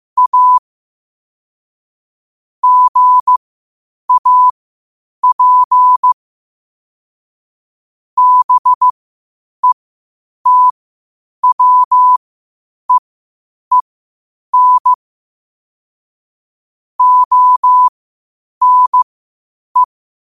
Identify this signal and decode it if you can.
Morse code, heard as a keyed tone, about 15 wpm, spaced as for 8 wpm, 1000 Hz. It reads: A GAP BETWEEN ONE
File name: qotdfiles / 8wpm